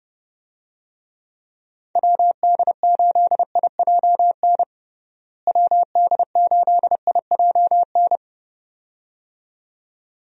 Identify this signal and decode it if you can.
Morse code, heard as a keyed tone, about 32 wpm, 700 Hz. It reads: WD8IJN WD8IJN
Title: USA Call Signs - 6 Characters - Repeated 2x - 30wpm